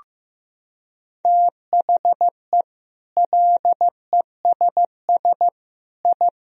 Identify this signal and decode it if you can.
Morse code, heard as a keyed tone, about 15 wpm, 700 Hz. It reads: THE LESS I
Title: Sentences from Top 1000 Words - Review Part 2 - 15wpm